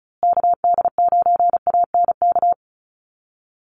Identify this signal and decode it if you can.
Morse code, heard as a keyed tone, about 34 wpm, 700 Hz. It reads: KD9ANK
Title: USA Call Signs Repeated 3x 35wpm